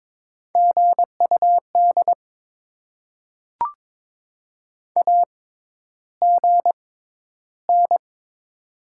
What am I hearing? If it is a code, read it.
Morse code, heard as a keyed tone, about 22 wpm, spaced as for 11 wpm, 690 Hz. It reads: GUD AGN
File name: Top 20 QSO Elements - 6x Character Spacing - 22wpm